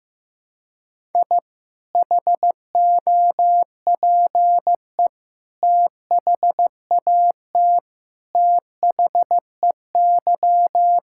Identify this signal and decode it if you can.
Morse code, heard as a keyed tone, about 15 wpm, 700 Hz. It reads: I HOPE THAT THEY
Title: Sentences from Top 1000 Words - Review Part 5 - 15wpm